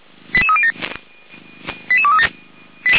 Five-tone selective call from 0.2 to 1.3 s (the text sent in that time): 88307